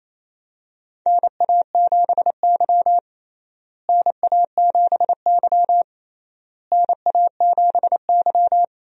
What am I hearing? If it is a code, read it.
Morse code, heard as a keyed tone, about 28 wpm, 700 Hz. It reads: NA7Y NA7Y NA7Y
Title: USA Call Signs - 4 Characters - Repeated 3x - 28wpm